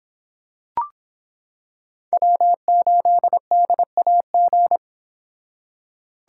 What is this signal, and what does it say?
Morse code, heard as a keyed tone, about 25 wpm, 700 Hz. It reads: W8DAG